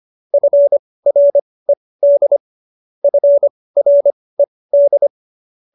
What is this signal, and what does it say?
Morse code, heard as a keyed tone, about 25 wpm, spaced as for 18 wpm, 570 Hz. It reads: FRED FRED
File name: qso222 18